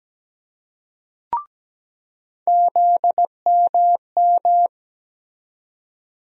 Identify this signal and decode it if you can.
Morse code, heard as a keyed tone, about 17 wpm, 700 Hz. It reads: ZMM
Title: USA Call Sign Postfixes - 3 Letters 17wpm